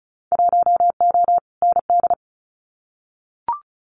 Morse code, heard as a keyed tone, about 34 wpm, 700 Hz. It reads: 1O ND
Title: Winter Field Day Exchanges 35wpm